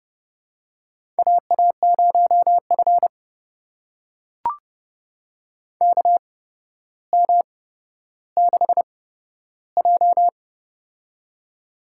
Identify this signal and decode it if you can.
Morse code, heard as a keyed tone, about 30 wpm, spaced as for 13 wpm, 700 Hz. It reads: AA0F KM6J
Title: USA Call Signs - 4 Characters - 8x Character Spacing - 30wpm